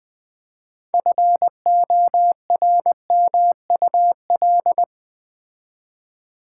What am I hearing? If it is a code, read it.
Morse code, heard as a keyed tone, about 20 wpm, 700 Hz. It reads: FORMUL
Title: Build A Word - 3 Letters At A Time - 9-Letter Words - 20wpm